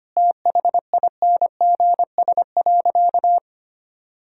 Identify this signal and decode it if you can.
Morse code, heard as a keyed tone, about 25 wpm, 700 Hz. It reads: THINGS.
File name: practice / 25wpm / Sentences with 4-Word Phrases - 25wpm